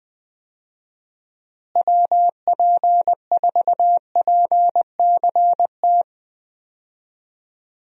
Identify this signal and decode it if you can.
Morse code, heard as a keyed tone, about 20 wpm, 700 Hz. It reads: WP4PCT
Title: USA Call Signs - 6 Characters - Repeated 3x - 20wpm